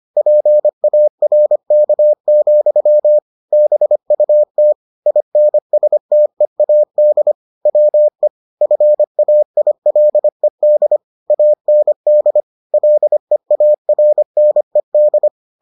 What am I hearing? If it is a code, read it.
Morse code, heard as a keyed tone, about 25 wpm, 600 Hz. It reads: PARK, BUT INSTEAD WE FAILED AND LEARNED